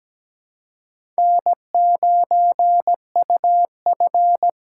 Morse code, heard as a keyed tone, about 17 wpm, 700 Hz. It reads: N9UF
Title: USA Call Signs - 4 Characters - 3x Character Spacing - 17wpm